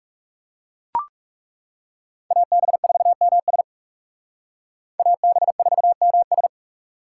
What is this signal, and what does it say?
Morse code, heard as a keyed tone, about 38 wpm, 700 Hz. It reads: AB4MS AB4MS